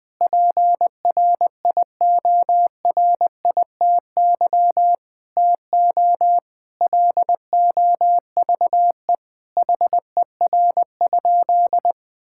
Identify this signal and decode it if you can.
Morse code, heard as a keyed tone, about 19 wpm, 700 Hz. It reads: PRIORITY TO LOVE HER?
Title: Sentences from Top 1600 Words 20wpm